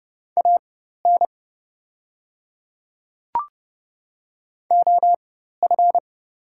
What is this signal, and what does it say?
Morse code, heard as a keyed tone, about 30 wpm, spaced as for 14 wpm, 700 Hz. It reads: AN OF
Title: 4x Character Spacing - Top 300 Words - 30wpm